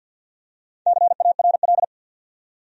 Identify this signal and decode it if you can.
Morse code, heard as a keyed tone, about 53 wpm, 700 Hz. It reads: CARL
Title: USA Common Names - Max 4 Letters - 50wpm